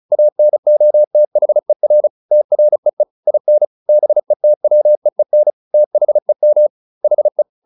Morse code, heard as a keyed tone, about 35 wpm, 600 Hz. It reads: ANOTHER TREE IN BETWEEN THEM HE